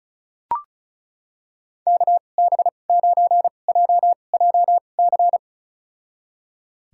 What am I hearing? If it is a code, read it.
Morse code, heard as a keyed tone, about 36 wpm, spaced as for 25 wpm, 700 Hz. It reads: KB9JJC
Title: USA Call Signs - 2x Character Spacing - 35wpm